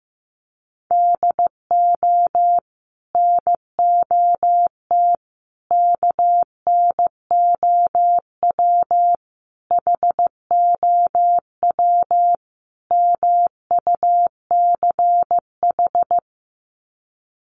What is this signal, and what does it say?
Morse code, heard as a keyed tone, about 15 wpm, 700 Hz. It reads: DO NOT KNOW HOW MUCH
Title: Sets of 5 Words Encore 15wpm